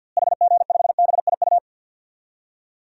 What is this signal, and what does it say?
Morse code, heard as a keyed tone, about 50 wpm, 700 Hz. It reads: HG5BIU
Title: DXCC-100 Call Signs - 6 Characters - Repeated 3x - 50wpm